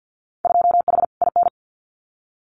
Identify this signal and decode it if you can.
Morse code, heard as a keyed tone, about 50 wpm, 700 Hz. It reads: 2H IN
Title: Winter Field Day Exchanges Encore 50wpm